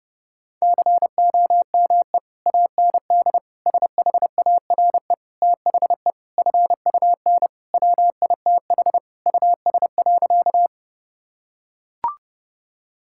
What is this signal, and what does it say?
Morse code, heard as a keyed tone, about 28 wpm, 700 Hz. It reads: COME AND SHARE THE FUN WITH US.
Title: Sentences from Top 1000 Words - Review Part 6 - 30wpm